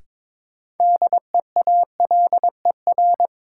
Morse code, heard as a keyed tone, about 22 wpm, 700 Hz. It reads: DEALER
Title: Sentences from Top 1100 Words - New Word Drill Alt - 22wpm